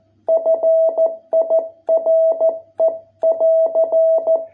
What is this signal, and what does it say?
Morse code, heard as a keyed tone, about 14 wpm, 660 Hz. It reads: FIRE <AR>